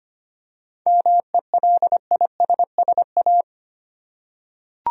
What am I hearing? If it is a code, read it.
Morse code, heard as a keyed tone, about 25 wpm, 700 Hz. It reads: MELISSA
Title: USA Common Names - 25wpm